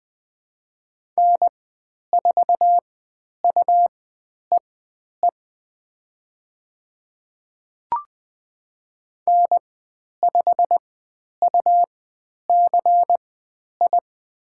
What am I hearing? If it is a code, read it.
Morse code, heard as a keyed tone, about 20 wpm, spaced as for 10 wpm, 700 Hz. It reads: N4UEE N5UCI
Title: 109 Lesson 18g - Letter m - Call Signs 20wpm